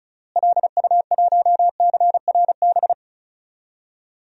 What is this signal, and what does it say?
Morse code, heard as a keyed tone, about 34 wpm, 700 Hz. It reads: LU1CRB
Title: DXCC-100 Call Signs - 6 Characters - Repeated 2x - 35wpm